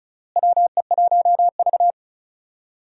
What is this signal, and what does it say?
Morse code, heard as a keyed tone, about 34 wpm, 700 Hz. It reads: WE1V
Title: USA Call Signs - 4 Characters - Repeated 3x - 35wpm